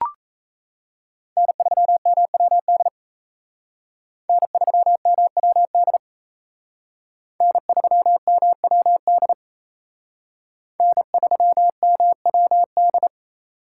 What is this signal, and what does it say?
Morse code, heard as a keyed tone, about 35 wpm, 700 Hz. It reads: N3MWD N3MWD N3MWD N3MWD
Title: Speed Racing - USA 1x3 Call Signs 28wpm